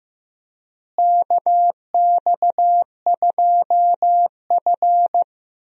Morse code, heard as a keyed tone, about 16 wpm, 700 Hz. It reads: KX2F
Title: USA Call Signs - 4 Characters - Repeated 2x - 15wpm